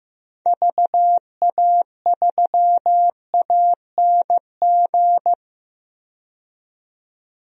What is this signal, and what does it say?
Morse code, heard as a keyed tone, about 15 wpm, 700 Hz. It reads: VA3ANG
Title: DXCC-100 Call Signs - 6 Characters - 15wpm